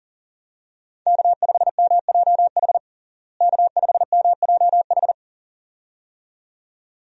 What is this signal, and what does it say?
Morse code, heard as a keyed tone, about 37 wpm, 700 Hz. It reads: K5MJH K5MJH
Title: USA Call Signs - 5 Characters - Repeated 2x - 40wpm